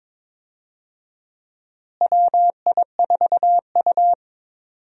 Morse code, heard as a keyed tone, about 22 wpm, 700 Hz. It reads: WI4U
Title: USA Call Signs - 4 Characters - Repeated 3x - 22wpm